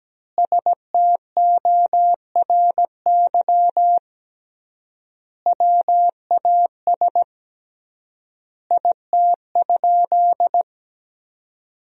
Morse code, heard as a keyed tone, about 17 wpm, 700 Hz. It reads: STORY WAS IT?
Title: 3x Word-Spacing - Sentences from Top 300 Words - 17wpm